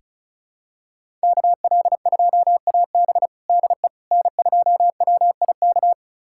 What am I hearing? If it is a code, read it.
Morse code, heard as a keyed tone, about 36 wpm, 700 Hz. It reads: KL2AB DE N2WIK
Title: USA Callsign DE Callsign - 5 Characters - 35wpm